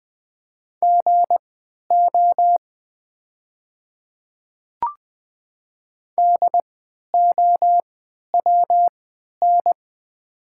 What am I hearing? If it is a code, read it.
Morse code, heard as a keyed tone, about 20 wpm, spaced as for 11 wpm, 700 Hz. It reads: GO DOWN
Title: 3x Character Spacing - Top 100 Words - 20wpm